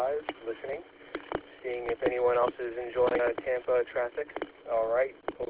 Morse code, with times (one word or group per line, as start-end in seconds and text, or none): none